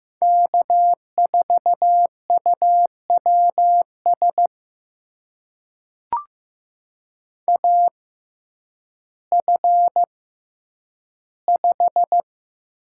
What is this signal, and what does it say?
Morse code, heard as a keyed tone, about 15 wpm, spaced as for 9 wpm, 700 Hz. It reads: K4UWS AF5
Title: USA Call Signs - 5 Characters - 6x Character Spacing - 15wpm